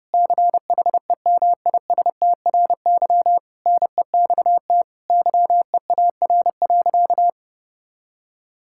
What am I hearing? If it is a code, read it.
Morse code, heard as a keyed tone, about 30 wpm, 700 Hz. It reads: CHEMISTRY NEXT YEAR.
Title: Sentences from Top 1000 Words - Review Part 6 - 30wpm